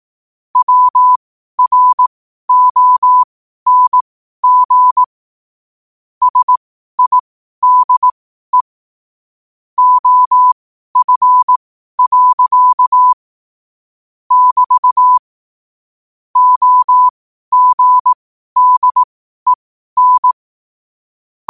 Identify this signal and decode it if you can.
Morse code, heard as a keyed tone, about 18 wpm, spaced as for 12 wpm, 1000 Hz. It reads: WRONG SIDE OF. <BT> OGDEN
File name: qotdfiles / 12wpm